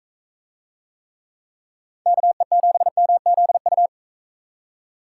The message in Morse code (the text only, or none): KE7MZU